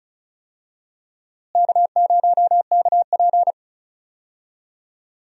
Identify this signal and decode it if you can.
Morse code, heard as a keyed tone, about 34 wpm, 700 Hz. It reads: K0KP
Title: USA Call Signs - 4 Characters - Repeated 3x - 35wpm